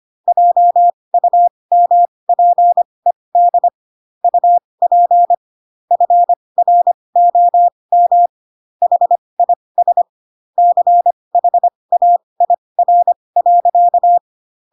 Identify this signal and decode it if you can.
Morse code, heard as a keyed tone, about 25 wpm, spaced as for 20 wpm, 700 Hz. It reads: JUMPED UP FROM HIS CHAIR.
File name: Pinocchio / Pinocchio ch8 15 25 ik1hge